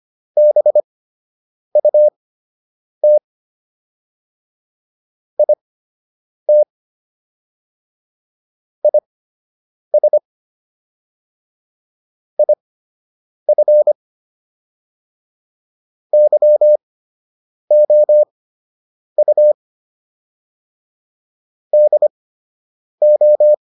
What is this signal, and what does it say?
Morse code, heard as a keyed tone, about 25 wpm, spaced as for 8 wpm, 600 Hz. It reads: BUT IT IS IF YOU DO